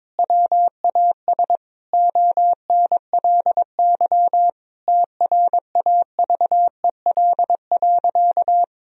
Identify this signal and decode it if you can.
Morse code, heard as a keyed tone, about 22 wpm, 700 Hz. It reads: WAS ONLY TRAVEL.
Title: Sentences from Top 1000 Words - Review Part 2 - 22wpm